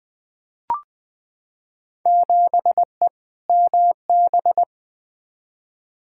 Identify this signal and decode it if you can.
Morse code, heard as a keyed tone, about 21 wpm, 700 Hz. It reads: 7E MB
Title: Field Day Exchanges Encore - 20wpm